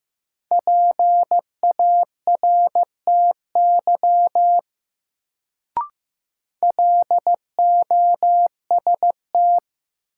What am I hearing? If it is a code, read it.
Morse code, heard as a keyed tone, about 15 wpm, 700 Hz. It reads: PARTY LOST
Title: Sentences from Top 1400 Words - New Word Drill Alt - 15wpm